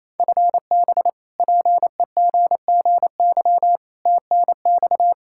Morse code, heard as a keyed tone, about 28 wpm, 700 Hz. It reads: FB PEGGY TNX
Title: Practice QSO - Exchange 3 - More Info - 28wpm